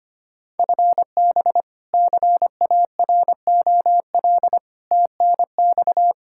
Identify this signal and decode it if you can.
Morse code, heard as a keyed tone, about 25 wpm, 700 Hz. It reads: FB CAROL TNX
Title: K1USN SST QSO Practice 25wpm